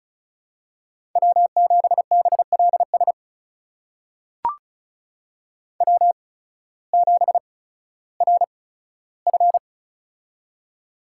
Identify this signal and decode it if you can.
Morse code, heard as a keyed tone, about 36 wpm, spaced as for 16 wpm, 700 Hz. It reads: W7BLS W7RF
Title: USA Call Signs - 8x Character Spacing - 35wpm